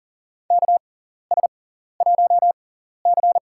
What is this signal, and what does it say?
Morse code, heard as a keyed tone, about 40 wpm, spaced as for 14 wpm, 700 Hz. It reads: KS1C
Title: USA Call Signs - 4 Characters - 6x Character Spacing - 40wpm